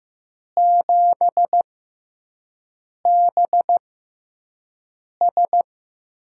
Morse code, heard as a keyed tone, about 15 wpm, spaced as for 5 wpm, 700 Hz. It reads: 7BS
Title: USA Call Signs - 5 Characters - 6x Character Spacing - 15wpm